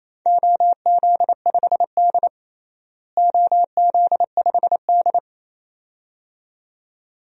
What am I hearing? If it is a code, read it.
Morse code, heard as a keyed tone, about 28 wpm, 700 Hz. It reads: OZ5D OZ5D
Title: DXCC-100 Call Signs - 4 Characters - Repeated 2x - 28wpm